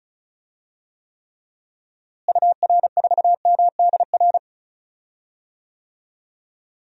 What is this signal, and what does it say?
Morse code, heard as a keyed tone, about 34 wpm, 700 Hz. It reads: UR4MDR
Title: DXCC-100 Call Signs - 6 Characters - Repeated 3x - 35wpm